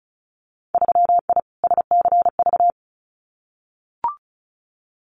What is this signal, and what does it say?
Morse code, heard as a keyed tone, about 34 wpm, 700 Hz. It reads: 3I SCV
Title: Winter Field Day Exchanges Encore 35wpm